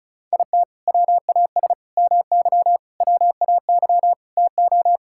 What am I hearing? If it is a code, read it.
Morse code, heard as a keyed tone, about 36 wpm, 700 Hz. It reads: IT WAS MY WAY TO
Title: Sentences from Top 1600 Words 35wpm